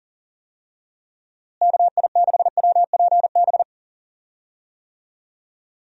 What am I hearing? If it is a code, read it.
Morse code, heard as a keyed tone, about 40 wpm, 700 Hz. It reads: KI6WPB
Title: USA Call Signs - 6 Characters - Repeated 3x - 40wpm